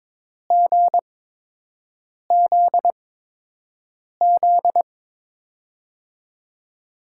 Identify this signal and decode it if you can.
Morse code, heard as a keyed tone, about 22 wpm, spaced as for 6 wpm, 700 Hz. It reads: GZZ
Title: USA Call Signs - 5 Characters - 8x Character Spacing - 22wpm